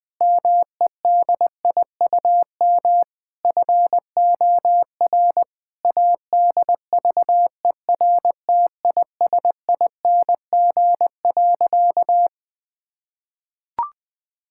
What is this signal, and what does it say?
Morse code, heard as a keyed tone, about 20 wpm, 700 Hz. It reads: MEDIUM FOR ADVERTISING.
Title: Sentences from Top 1000 Words - Review Part 6 - 20wpm